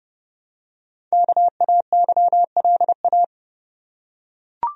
Morse code, heard as a keyed tone, about 30 wpm, 700 Hz. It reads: KAYLA
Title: USA Common Names - Max 5 Letters - 30wpm